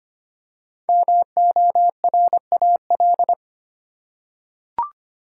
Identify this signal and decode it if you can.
Morse code, heard as a keyed tone, about 25 wpm, 700 Hz. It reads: MORAL
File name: 5 Letter Words - First 3 Letters Given - 25wpm